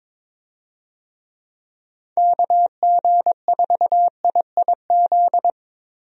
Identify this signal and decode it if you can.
Morse code, heard as a keyed tone, about 22 wpm, 700 Hz. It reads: KG4IIZ
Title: USA Call Signs - 6 Characters - 22wpm